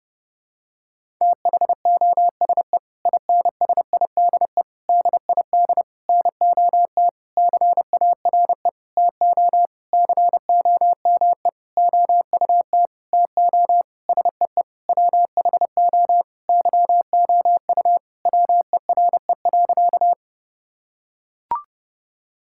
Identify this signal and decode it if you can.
Morse code, heard as a keyed tone, about 30 wpm, 700 Hz. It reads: THOSE INSIDE DID NOT CARE TO COME OUT TO SEE WHO YOU WERE.
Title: Sentences from Top 1000 Words - Review Part 6 - 30wpm